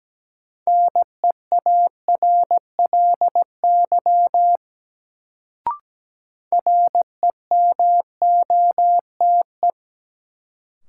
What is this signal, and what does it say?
Morse code, heard as a keyed tone, about 17 wpm, 700 Hz. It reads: NEARLY REMOTE
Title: Sentences from Top 1300 Words - New Word Drill - 17wpm